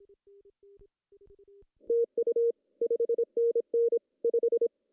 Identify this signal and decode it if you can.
Morse code, heard as a keyed tone, about 25 wpm, 440 Hz. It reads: TU 5NN 5
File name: CW / cqww-cw-k5tr